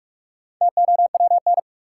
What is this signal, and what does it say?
Morse code, heard as a keyed tone, about 46 wpm, 700 Hz. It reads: TOWN
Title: Sentences from Top 500 Words - New Word Drill - 45wpm